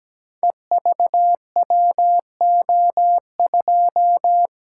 Morse code, heard as a keyed tone, about 17 wpm, 700 Hz. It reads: EVWO2
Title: Code Groups - 5 Letter:Numbers - 17wpm